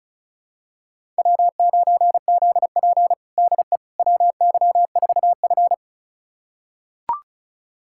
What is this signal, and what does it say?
Morse code, heard as a keyed tone, about 36 wpm, 700 Hz. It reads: W9ZP DE WY4F
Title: USA Callsign DE Callsign - 4 Characters - 35wpm